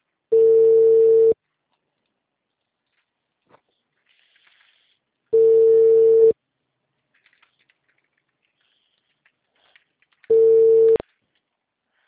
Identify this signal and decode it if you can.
Ringback tone (China, 450 Hz).